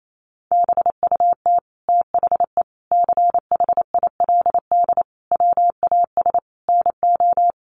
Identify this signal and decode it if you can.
Morse code, heard as a keyed tone, about 28 wpm, 700 Hz. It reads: BUT THE CHILD WAS NO
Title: Sentences from Top 1500 Words 28wpm